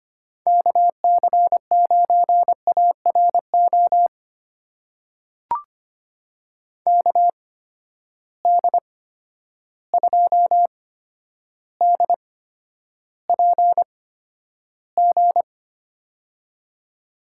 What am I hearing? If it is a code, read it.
Morse code, heard as a keyed tone, about 25 wpm, spaced as for 11 wpm, 700 Hz. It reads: KC9ARO KD2DPG
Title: USA Call Signs - 6 Characters - 8x Character Spacing - 25wpm